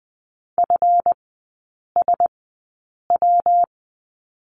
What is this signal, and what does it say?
Morse code, heard as a keyed tone, about 20 wpm, spaced as for 8 wpm, 700 Hz. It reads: FSW
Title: 138 Lesson 23e - Letter f - Three Characters 20wpm